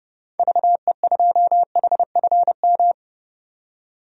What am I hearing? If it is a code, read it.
Morse code, heard as a keyed tone, about 30 wpm, 700 Hz. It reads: VE2HFM
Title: DXCC-100 Call Signs - 6 Characters - 30wpm